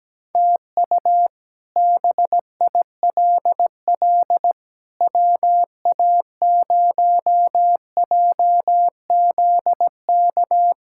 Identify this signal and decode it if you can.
Morse code, heard as a keyed tone, about 17 wpm, 700 Hz. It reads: TU BILL WA0JZK
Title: K1USN SST QSO Practice - 17wpm